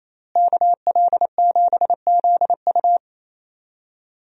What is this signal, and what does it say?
Morse code, heard as a keyed tone, about 28 wpm, 700 Hz. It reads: KL7ZU
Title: USA Call Signs - 5 Characters - Repeated 2x - 28wpm